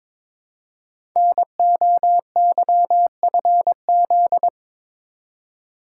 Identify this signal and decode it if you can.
Morse code, heard as a keyed tone, about 22 wpm, 700 Hz. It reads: NOYFZ
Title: Code Groups - 5 Letters - 22wpm